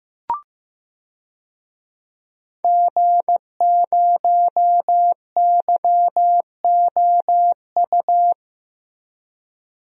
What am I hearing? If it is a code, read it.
Morse code, heard as a keyed tone, about 15 wpm, 700 Hz. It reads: G0YOU
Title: DXCC-100 Call Signs - 5 Characters - Repeated 2x - 15wpm